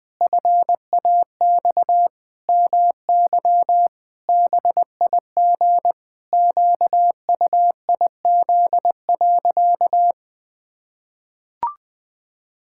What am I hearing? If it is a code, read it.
Morse code, heard as a keyed tone, about 20 wpm, 700 Hz. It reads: FAX MY BIG QUIZ.
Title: Pangram Challenge v2 - 20wpm